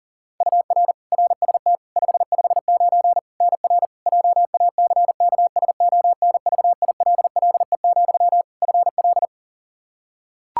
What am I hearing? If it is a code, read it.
Morse code, heard as a keyed tone, about 37 wpm, 700 Hz. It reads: UR RST 559 NR JACKSONVILLE, FL
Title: Practice QSO - Exchange 1 - Beginning - 40wpm